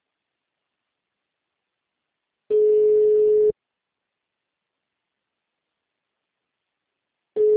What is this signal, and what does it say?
Ringback tone (Europe, 425 Hz).
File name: out- -2030- - - .46